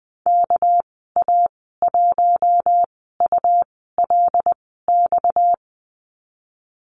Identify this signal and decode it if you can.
Morse code, heard as a keyed tone, about 20 wpm, spaced as for 14 wpm, 700 Hz. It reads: KA1ULX
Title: 231 Lesson 37g - Letter x - Call Signs 20wpm